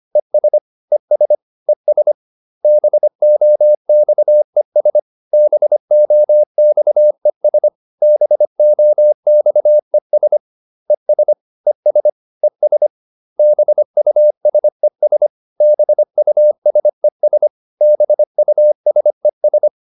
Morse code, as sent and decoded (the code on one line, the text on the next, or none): . ... / . ... / . ... / -... --- -..- . ... / -... --- -..- . ... / -... --- -..- . ... / . ... / . ... / . ... / -... ..- ... ... / -... ..- ... ... / -... ..- ... ...
ES ES ES BOXES BOXES BOXES ES ES ES BUSES BUSES BUSES